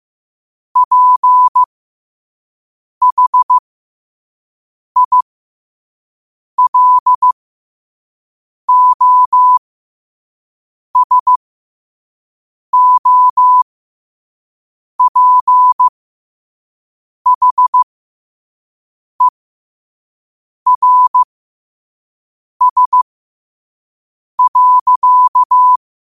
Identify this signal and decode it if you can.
Morse code, heard as a keyed tone, about 15 wpm, spaced as for 5 wpm, 1000 Hz. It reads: PHILOSOPHERS.
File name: qotdfiles / 5wpm